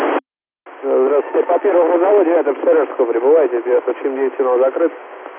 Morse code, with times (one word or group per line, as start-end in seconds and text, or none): none